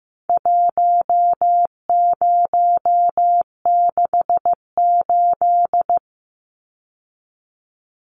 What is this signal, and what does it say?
Morse code, heard as a keyed tone, about 14 wpm, 700 Hz. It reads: 1068
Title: Numbers - 4-Digit 15wpm